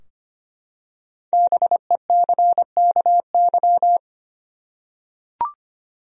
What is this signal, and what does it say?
Morse code, heard as a keyed tone, about 25 wpm, 700 Hz. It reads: BECKY
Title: USA Common Names - Max 5 Letters - 25wpm